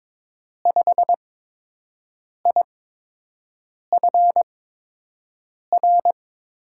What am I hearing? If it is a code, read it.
Morse code, heard as a keyed tone, about 22 wpm, spaced as for 6 wpm, 700 Hz. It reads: 5IFR
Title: USA Call Signs - 5 Characters - 8x Character Spacing - 22wpm